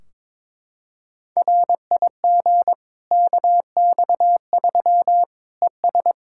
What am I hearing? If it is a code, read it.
Morse code, heard as a keyed tone, about 22 wpm, 700 Hz. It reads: RIG KX3 ES
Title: Practice QSO - Exchange 2 - Info - 22wpm